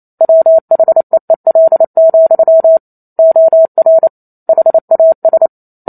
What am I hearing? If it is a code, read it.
Morse code, heard as a keyed tone, about 28 wpm, 650 Hz. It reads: WHEEL, OR HAS
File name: Chapter 2 - 28 WPM